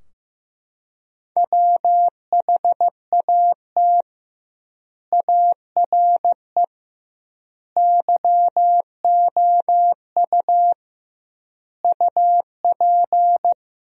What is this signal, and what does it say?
Morse code, heard as a keyed tone, about 16 wpm, 700 Hz. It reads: WHAT ARE YOU UP
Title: 2x Word-Spacing - Sentences from Top 100 Words - 15wpm